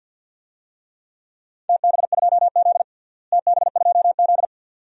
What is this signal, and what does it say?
Morse code, heard as a keyed tone, about 50 wpm, 700 Hz. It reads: TB2Z TB2Z
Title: DXCC-100 Call Signs - 4 Characters - Repeated 2x - 50wpm